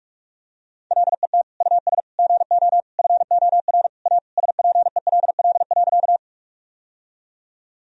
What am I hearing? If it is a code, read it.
Morse code, heard as a keyed tone, about 41 wpm, 700 Hz. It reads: LET US GO FOR A SPELL.
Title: Sentences from Top 1000 Words - Review Part 1 - 45wpm